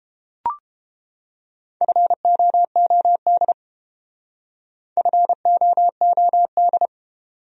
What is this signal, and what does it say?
Morse code, heard as a keyed tone, about 29 wpm, 700 Hz. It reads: FOOD FOOD